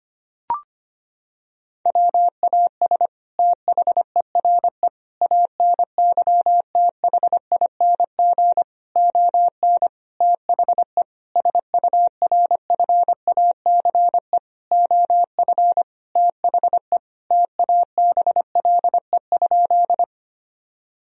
Morse code, heard as a keyed tone, about 25 wpm, 700 Hz. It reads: WAS THERE ANYTHING ON THE SURFACE OF THE TABLE?
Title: Sentences from Top 1000 Words - Review Part 3 - 25wpm